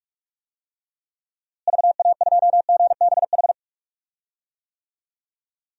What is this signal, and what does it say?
Morse code, heard as a keyed tone, about 46 wpm, 700 Hz. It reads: VA2GBH
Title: DXCC-100 Call Signs - 6 Characters - Repeated 3x - 45wpm